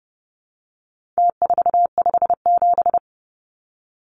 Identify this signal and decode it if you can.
Morse code, heard as a keyed tone, about 30 wpm, 700 Hz. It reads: T457
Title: Numbers - Contest Style - 4-Digit 30wpm